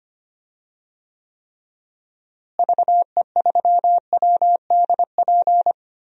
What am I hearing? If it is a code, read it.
Morse code, heard as a keyed tone, about 25 wpm, 700 Hz. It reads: VE3WDP
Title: DXCC-100 Call Signs - 6 Characters - Repeated 3x - 25wpm